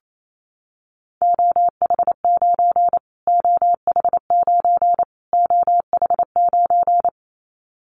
Morse code, heard as a keyed tone, about 28 wpm, 700 Hz. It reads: OH9 OH9 OH9
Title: Callsign Prefixes for Top 25 DXCC - Repeated 3x - 28wpm